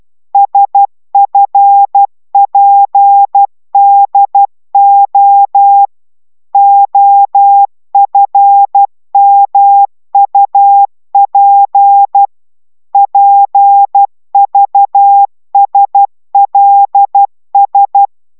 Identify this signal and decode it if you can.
Morse code, heard as a keyed tone, about 12 wpm, 800 Hz. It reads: SFPDO OFMUP PVSLS